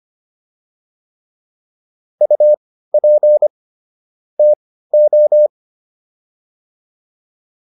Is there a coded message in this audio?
Yes: Morse code from 2.2 to 5.5 s, heard as a keyed tone, about 25 wpm, spaced as for 15 wpm, 600 Hz. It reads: UP TO